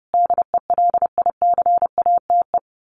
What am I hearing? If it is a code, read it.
Morse code, heard as a keyed tone, about 30 wpm, 700 Hz. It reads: DELICATE
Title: A Sound Of Thunder 30wpm